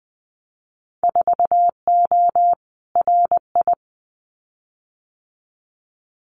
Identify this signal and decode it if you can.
Morse code, heard as a keyed tone, about 20 wpm, 700 Hz. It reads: 4O RI
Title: Winter Field Day Exchanges Encore 20wpm